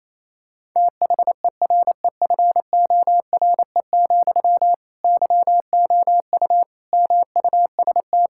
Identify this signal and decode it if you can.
Morse code, heard as a keyed tone, about 28 wpm, 700 Hz. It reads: THEREFORE, YOU MUST
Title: Sentences from Top 1000 Words - Review Part 5 - 28wpm